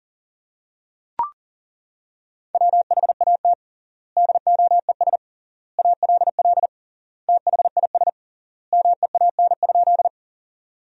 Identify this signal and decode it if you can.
Morse code, heard as a keyed tone, about 40 wpm, 700 Hz. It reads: WHAT DOES ALL THIS MEAN?
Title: 3x Word-Spacing - Sentences from Top 200 Words - 40wpm